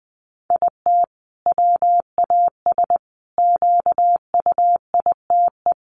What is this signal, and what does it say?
Morse code, heard as a keyed tone, about 20 wpm, 700 Hz. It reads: IT WAS QUITE
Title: Sentences from Top 1200 Words 20wpm